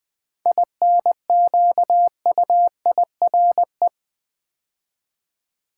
Morse code, heard as a keyed tone, about 20 wpm, 700 Hz. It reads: INQUIRE
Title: 7 Letter Words - First 3 Letters Given - 20wpm